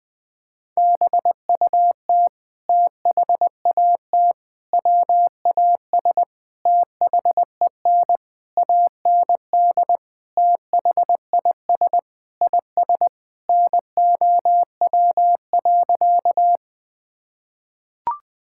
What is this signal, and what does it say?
Morse code, heard as a keyed tone, about 21 wpm, 700 Hz. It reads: BUT THAT WAS THEN AND THIS IS NOW.
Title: Sentences with 8-Word Phrases - 20wpm